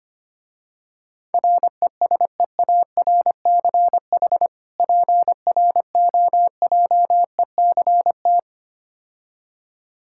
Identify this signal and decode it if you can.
Morse code, heard as a keyed tone, about 25 wpm, 700 Hz. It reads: RESEARCH PROJECT
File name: practice / 25wpm / Difficult Words - Speed-Racing with Hints - 25wpm